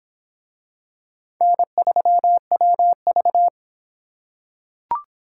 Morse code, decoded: N3WV

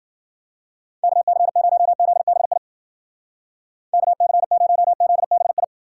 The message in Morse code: KC0ZBI KC0ZBI

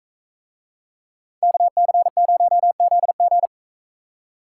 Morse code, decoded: KC0ZG

